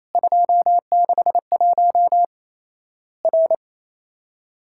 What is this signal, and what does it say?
Morse code, heard as a keyed tone, about 27 wpm, 700 Hz. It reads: 261 R
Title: Worked All Europe DX Contest - QTC Exchange - 28wpm